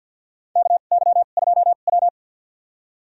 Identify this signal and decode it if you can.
Morse code, heard as a keyed tone, about 51 wpm, spaced as for 36 wpm, 700 Hz. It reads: KY2W
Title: USA Call Signs - 4 Characters - 2x Character Spacing - 50wpm